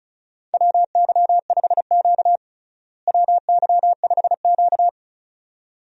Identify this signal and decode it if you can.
Morse code, heard as a keyed tone, about 36 wpm, 700 Hz. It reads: WY5Q WY5Q
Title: USA Call Signs - 4 Characters - Repeated 2x - 35wpm